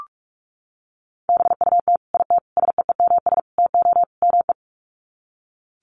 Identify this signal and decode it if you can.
Morse code, heard as a keyed tone, about 44 wpm, 700 Hz. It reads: BUT IT SEEMS TO ME
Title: Sets of 5 Words Encore2 45wpm